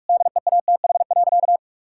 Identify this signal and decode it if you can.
Morse code, heard as a keyed tone, about 42 wpm, 700 Hz. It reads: DEATH.
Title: Sentences from Top 1600 Words 45wpm